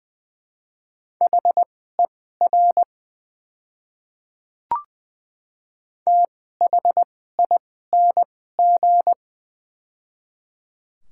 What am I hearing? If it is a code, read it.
Morse code, heard as a keyed tone, about 20 wpm, spaced as for 14 wpm, 700 Hz. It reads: HER THING
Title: 2x Character Spacing - Top 100 Words - 20wpm